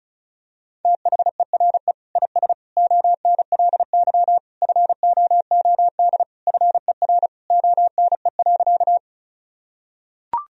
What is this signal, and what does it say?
Morse code, heard as a keyed tone, about 36 wpm, 700 Hz. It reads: THERE IS ONLY FOOD FER ONE.
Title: Sentences from Top 300 Words and Abbreviations 35wpm